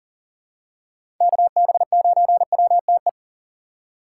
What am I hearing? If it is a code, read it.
Morse code, heard as a keyed tone, about 40 wpm, 700 Hz. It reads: KB9WTE
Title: USA Call Signs - 6 Characters - 3x Character Spacing - 40wpm